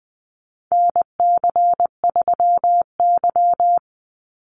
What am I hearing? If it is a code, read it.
Morse code, heard as a keyed tone, about 20 wpm, 700 Hz. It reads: NC3Y
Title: Call Signs - 20wpm